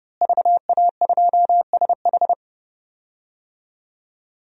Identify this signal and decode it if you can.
Morse code, heard as a keyed tone, about 30 wpm, 700 Hz. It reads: VA2SH
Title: DXCC-100 Call Signs - 5 Characters - Repeated 3x - 30wpm